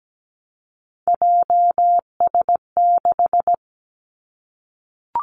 Morse code, heard as a keyed tone, about 17 wpm, 700 Hz. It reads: JS6
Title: Three Challenging Letter-Number Combinations 17wpm